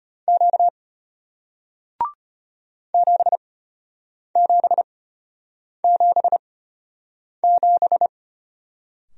Morse code, heard as a keyed tone, about 34 wpm, spaced as for 8 wpm, 690 Hz. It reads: Q 7777